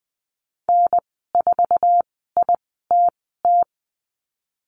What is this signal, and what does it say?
Morse code, heard as a keyed tone, about 20 wpm, spaced as for 15 wpm, 700 Hz. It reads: N4ITT
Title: 062 Lesson 11g - Letter h - Call Signs 20wpm